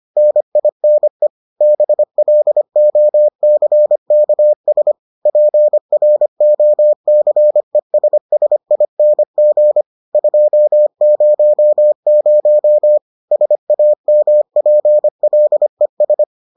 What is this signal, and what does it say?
Morse code, heard as a keyed tone, about 25 wpm, 600 Hz. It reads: NINE BLOCKS PROCESSING 200 SAMPLES